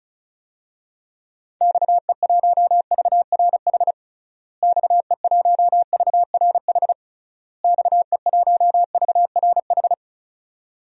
Morse code, heard as a keyed tone, about 36 wpm, 700 Hz. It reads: XE1VRH XE1VRH XE1VRH
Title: DXCC-100 Call Signs - 6 Characters - Repeated 3x - 35wpm